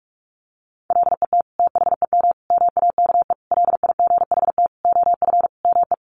Morse code, heard as a keyed tone, about 44 wpm, 700 Hz. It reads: LET THEM MAKE LIGHT OF ME
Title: Sentences from Top 1300 Words 45wpm